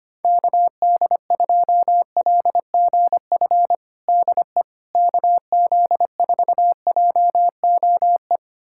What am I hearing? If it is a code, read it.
Morse code, heard as a keyed tone, about 25 wpm, 700 Hz. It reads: KD2LGF DE KZ4JOE
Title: USA Callsign DE Callsign - 6 Characters - 25wpm